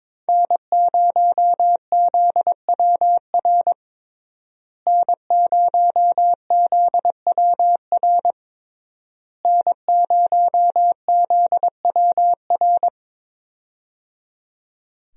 Morse code, heard as a keyed tone, about 22 wpm, 700 Hz. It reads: N0ZWR N0ZWR N0ZWR
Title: USA Call Signs - 5 Characters - Repeated 3x - 22wpm